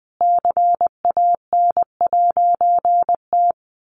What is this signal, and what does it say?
Morse code, heard as a keyed tone, about 20 wpm, 700 Hz. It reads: CAN'T
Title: A Sound Of Thunder 20wpm